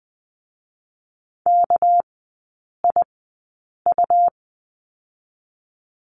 Morse code, heard as a keyed tone, about 20 wpm, spaced as for 8 wpm, 700 Hz. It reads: KIU
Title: 198 Lesson 32e - Letter k - Three Characters 20wpm